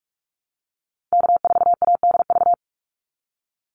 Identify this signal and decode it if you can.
Morse code, heard as a keyed tone, about 44 wpm, 700 Hz. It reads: K4ADV
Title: NA SOTA Activator Callsigns 45wpm